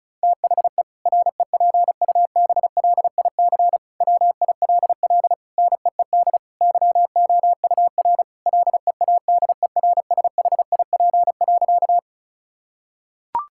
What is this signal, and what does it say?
Morse code, heard as a keyed tone, about 34 wpm, 700 Hz. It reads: THE REPUBLIC WILL NEED YOUR LEADERSHIP.
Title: Sentences from Top 1600 Words 35wpm